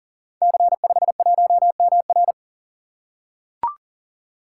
Morse code, heard as a keyed tone, about 41 wpm, 700 Hz. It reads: C51MR
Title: Code Groups - 5 Letter:Numbers - 40wpm